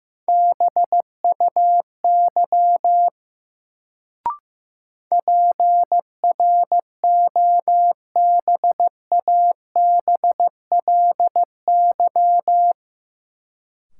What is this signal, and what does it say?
Morse code, heard as a keyed tone, about 15 wpm, 700 Hz. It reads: BUY PROBABLY
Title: Sentences from Top 700 Words - New Word Drill - 15wpm